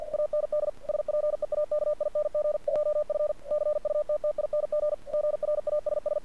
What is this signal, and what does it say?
Morse code, heard as a keyed tone, about 47 wpm, 600 Hz. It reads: AND SPEAKING JF CUTTING GRASS
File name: archived / CRSnet